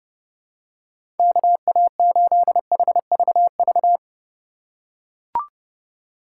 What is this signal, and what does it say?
Morse code, heard as a keyed tone, about 29 wpm, 700 Hz. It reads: KA8HVV
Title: USA Call Signs - 6 Characters - 6x Character Spacing - 30wpm